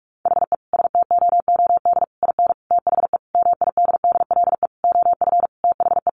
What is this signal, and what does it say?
Morse code, heard as a keyed tone, about 44 wpm, 700 Hz. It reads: HE STOOD IN THE MIDDLE OF THE
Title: Sentences from Top 1400 Words 45wpm